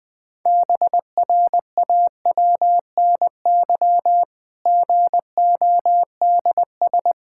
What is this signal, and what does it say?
Morse code, heard as a keyed tone, about 20 wpm, 700 Hz. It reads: BRAWNY GODS
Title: Pangram Challenge v2 - 20wpm